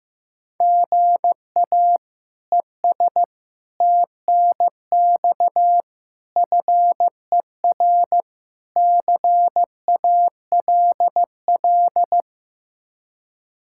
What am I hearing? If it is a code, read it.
Morse code, heard as a keyed tone, about 15 wpm, 700 Hz. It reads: GA ES TNX FER CALL
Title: K1USN SST QSO Practice 15wpm